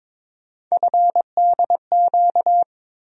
Morse code, heard as a keyed tone, about 22 wpm, 700 Hz. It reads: FDQ
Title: USA Call Sign Postfixes - 3 Letters 22wpm